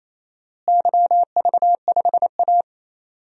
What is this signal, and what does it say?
Morse code, heard as a keyed tone, about 28 wpm, 700 Hz. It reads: YV5A
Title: DXCC-100 Call Signs - 4 Characters - Repeated 2x - 28wpm